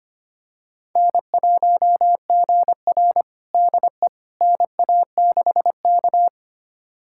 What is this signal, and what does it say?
Morse code, heard as a keyed tone, about 25 wpm, 700 Hz. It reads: N1GR DE NA6K
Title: USA Callsign DE Callsign - 4 Characters - 25wpm